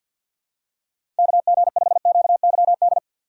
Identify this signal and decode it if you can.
Morse code, heard as a keyed tone, about 53 wpm, 700 Hz. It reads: KG5QYD
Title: USA Call Signs - 4x Character Spacing - 50wpm